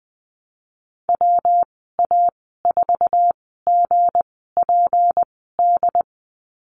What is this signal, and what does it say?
Morse code, heard as a keyed tone, about 20 wpm, spaced as for 15 wpm, 700 Hz. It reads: WA4GPD
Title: 186 Lesson 30g - Letter b - Call Signs 20wpm